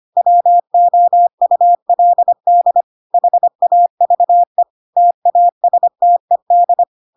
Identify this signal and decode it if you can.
Morse code, heard as a keyed tone, about 25 wpm, 700 Hz. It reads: WOULD HAVE TASTED
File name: Pinocchio / Pinocchio ch5 25 25 ik1hge